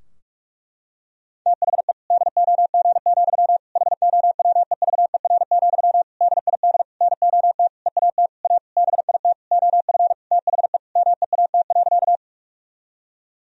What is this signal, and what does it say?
Morse code, heard as a keyed tone, about 46 wpm, 700 Hz. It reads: THE DOG, HOWEVER, DID NOT EAT A BIT OF THE MEAT.
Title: Sentences from Top 1000 Words - Review Part 3 - 45wpm